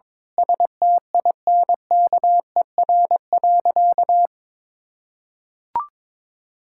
Morse code, heard as a keyed tone, about 21 wpm, 700 Hz. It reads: STINKER.
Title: Pangram Challenge v2 - 22wpm